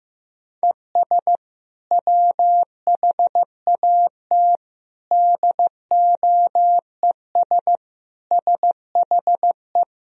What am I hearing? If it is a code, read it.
Morse code, heard as a keyed tone, about 15 wpm, 700 Hz. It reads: ES WHAT DOES SHE
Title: Sentences from Top 400 Words and Abbreviations 15wpm